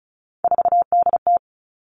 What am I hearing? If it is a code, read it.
Morse code, heard as a keyed tone, about 34 wpm, 700 Hz. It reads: 4DT
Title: All Cut Numbers - 3-Digit 35wpm